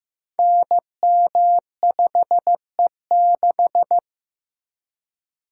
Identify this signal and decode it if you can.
Morse code, heard as a keyed tone, about 15 wpm, 700 Hz. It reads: NM5E6
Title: Code Groups - 5 Letter:Numbers - 15wpm